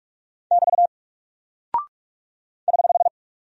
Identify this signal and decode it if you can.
Morse code, heard as a keyed tone, about 44 wpm, 700 Hz. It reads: <BT> <HH>
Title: Single Character (Letter:Number:Punctuation:Prosign) 45wpm